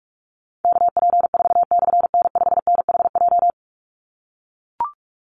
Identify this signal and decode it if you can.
Morse code, heard as a keyed tone, about 44 wpm, 700 Hz. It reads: KP4/N5NHJ
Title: Call Signs - 45wpm